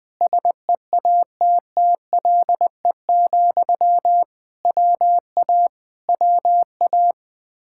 Morse code, heard as a keyed tone, about 20 wpm, 700 Hz. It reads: SEATTLE, WA WA
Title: Speed-Racing Common QSO Phrases - 17wpm